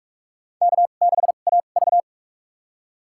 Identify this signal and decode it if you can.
Morse code, heard as a keyed tone, about 46 wpm, spaced as for 32 wpm, 700 Hz. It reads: K6AV
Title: USA Call Signs - 2x Character Spacing - 45wpm